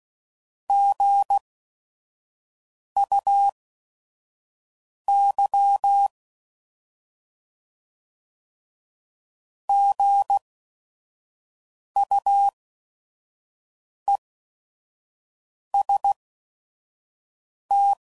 Morse code, heard as a keyed tone, about 16 wpm, spaced as for 5 wpm, 800 Hz. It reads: GUY GUEST
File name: Prefixes / By First Letter / Gs